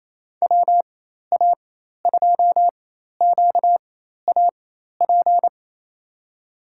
Morse code, heard as a keyed tone, about 28 wpm, spaced as for 13 wpm, 700 Hz. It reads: WA2QAP
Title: USA Call Signs - 6 Characters - 4x Character Spacing - 28wpm